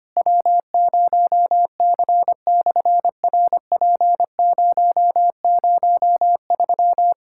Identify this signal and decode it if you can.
Morse code, heard as a keyed tone, about 25 wpm, 700 Hz. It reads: W0C/RP003
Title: SOTA QSO Practice 25wpm